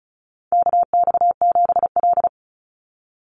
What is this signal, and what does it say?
Morse code, heard as a keyed tone, about 34 wpm, 700 Hz. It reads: KX7L
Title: NA SOTA Chaser Callsigns 35wpm